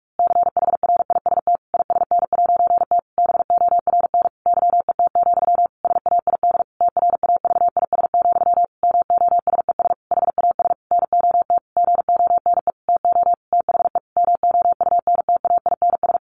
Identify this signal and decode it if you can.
Morse code, heard as a keyed tone, about 44 wpm, 700 Hz. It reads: CHRIST ISN'T BORN YET, SAID TRAVIS, MOSES HAS NOT GONE TO THE MOUNTAINS
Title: A Sound Of Thunder 45wpm